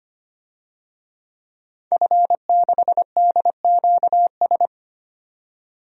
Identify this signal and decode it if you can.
Morse code, heard as a keyed tone, about 25 wpm, 700 Hz. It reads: F6DQS
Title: DXCC-100 Call Signs - 5 Characters - 25wpm